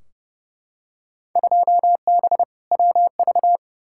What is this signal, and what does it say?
Morse code, heard as a keyed tone, about 30 wpm, 700 Hz. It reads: 2B WV
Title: Field Day Exchanges Encore - 30wpm